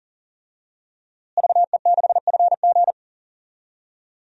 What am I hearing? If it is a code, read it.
Morse code, heard as a keyed tone, about 40 wpm, 700 Hz. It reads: VE6FG
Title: DXCC-100 Call Signs - 5 Characters - Repeated 3x - 40wpm